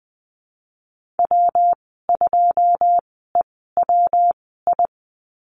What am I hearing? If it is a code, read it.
Morse code, heard as a keyed tone, about 20 wpm, spaced as for 14 wpm, 700 Hz. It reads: W2EWI
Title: 173 Lesson 28g - Number 9 - Call Signs 20wpm